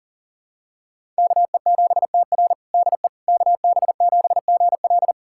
Morse code, heard as a keyed tone, about 37 wpm, 700 Hz. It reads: KE7TR DE KB7GL
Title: USA Callsign DE Callsign - 5 Characters - 40wpm